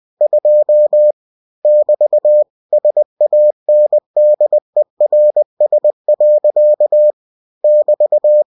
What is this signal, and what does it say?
Morse code, heard as a keyed tone, about 20 wpm, 600 Hz. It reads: 2 <BT> SANDERS. <BT>
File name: cw-books / winnie-the-pooh / winnie-20wpm-page-0002